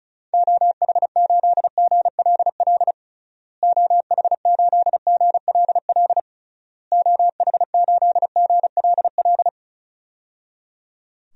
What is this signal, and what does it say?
Morse code, heard as a keyed tone, about 34 wpm, 700 Hz. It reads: OH8GLL OH8GLL OH8GLL
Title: DXCC-100 Call Signs - 6 Characters - Repeated 3x - 35wpm